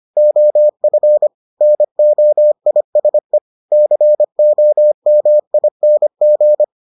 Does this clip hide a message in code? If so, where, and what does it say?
Yes: Morse code from 0.2 to 6.6 s, heard as a keyed tone, about 25 wpm, 600 Hz. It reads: OF NOISE COMING